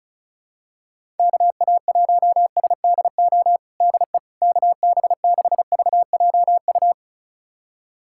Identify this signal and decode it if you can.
Morse code, heard as a keyed tone, about 36 wpm, 700 Hz. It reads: KA1SDO DE KB6VJU
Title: USA Callsign DE Callsign - 6 Characters - 35wpm